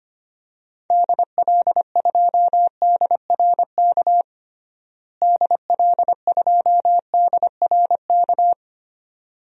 Morse code, heard as a keyed tone, about 25 wpm, 700 Hz. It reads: DL2DRK DL2DRK
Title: DXCC-100 Call Signs - 6 Characters - Repeated 2x - 25wpm